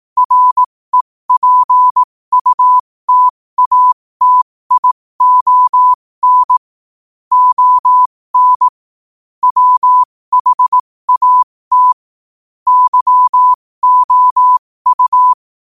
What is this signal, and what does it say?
Morse code, heard as a keyed tone, about 18 wpm, spaced as for 15 wpm, 1000 Hz. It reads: REPUTATION ON WHAT YOU
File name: qotdfiles / 15wpm